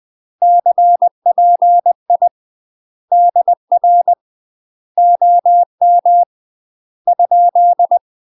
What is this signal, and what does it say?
Morse code, heard as a keyed tone, about 20 wpm, 700 Hz. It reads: CPI DR OM ?
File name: UK MASTER QSOs TRAINING/20WPM